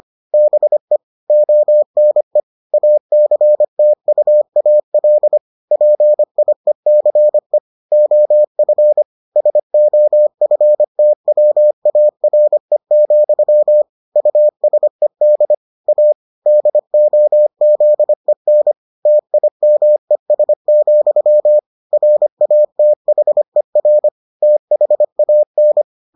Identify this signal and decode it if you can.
Morse code, heard as a keyed tone, about 25 wpm, 600 Hz. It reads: BE ONE ACTUAL PIECE OF SOFTWARE, USED A DOZEN TIMES, RATHER THAN